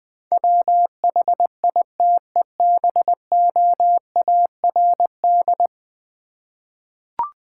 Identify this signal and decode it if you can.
Morse code, heard as a keyed tone, about 21 wpm, 700 Hz. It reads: WHITEBOARD
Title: Easy Words with Hints - 20wpm